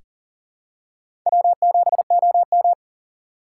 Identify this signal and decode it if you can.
Morse code, heard as a keyed tone, about 39 wpm, 700 Hz. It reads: W7OM
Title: USA Call Signs - 4x Character Spacing - 40wpm